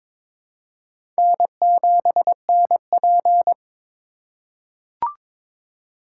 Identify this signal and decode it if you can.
Morse code, heard as a keyed tone, about 22 wpm, 700 Hz. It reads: N7NP
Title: USA Call Signs - 3x Character Spacing - 22wpm